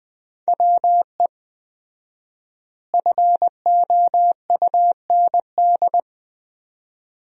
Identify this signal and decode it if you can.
Morse code, heard as a keyed tone, about 20 wpm, 700 Hz. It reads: WE FOUND
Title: 4x Word-Spacing - Sentences from Top 300 Words - 20wpm